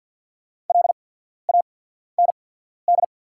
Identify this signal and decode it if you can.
Morse code, heard as a keyed tone, about 48 wpm, spaced as for 14 wpm, 690 Hz. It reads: LAND
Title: 8x Character Spacing - Top 500 Words - 50wpm